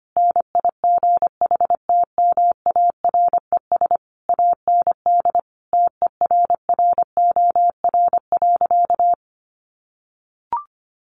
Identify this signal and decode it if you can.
Morse code, heard as a keyed tone, about 25 wpm, 700 Hz. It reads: NIGHTMARES AND TERROR.
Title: A Sound Of Thunder 25wpm